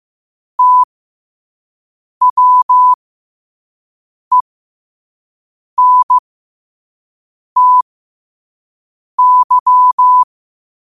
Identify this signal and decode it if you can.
Morse code, heard as a keyed tone, about 15 wpm, spaced as for 5 wpm, 1000 Hz. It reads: TWENTY
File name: qotdfiles / 5wpm